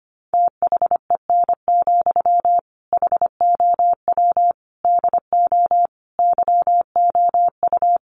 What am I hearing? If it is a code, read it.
Morse code, heard as a keyed tone, about 25 wpm, 700 Hz. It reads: THEN, HOW DO YOU
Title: Sentences from Top 100 Words 25wpm